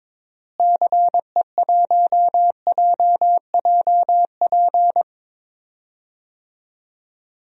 Morse code, heard as a keyed tone, about 22 wpm, 700 Hz. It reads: CE1JJP
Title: DXCC-100 Call Signs - 6 Characters - Repeated 2x - 22wpm